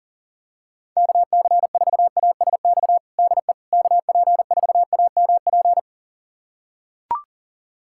Morse code, heard as a keyed tone, about 38 wpm, 700 Hz. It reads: KC4ASX DE KP4AMP
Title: USA Callsign DE Callsign 40wpm